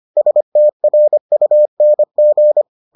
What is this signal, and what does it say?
Morse code, heard as a keyed tone, about 25 wpm, 600 Hz. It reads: STRUNG